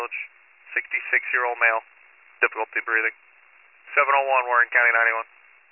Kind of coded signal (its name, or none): Emergency Alert System (SAME) tones